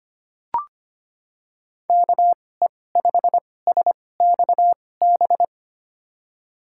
Morse code, heard as a keyed tone, about 25 wpm, spaced as for 18 wpm, 700 Hz. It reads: KE5SXB
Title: USA Call Signs - 6 Characters - 2x Character Spacing - 25wpm